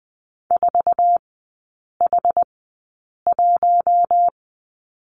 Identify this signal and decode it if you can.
Morse code, heard as a keyed tone, about 20 wpm, spaced as for 8 wpm, 700 Hz. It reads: 4H1
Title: 087 Lesson 15e - Number 5 - Three Characters 20wpm